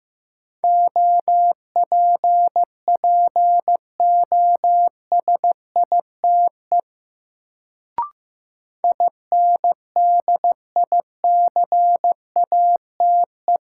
Morse code, heard as a keyed tone, about 15 wpm, 700 Hz. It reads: OPPOSITE INDICATE
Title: Sentences from Top 1300 Words - New Word Drill - 15wpm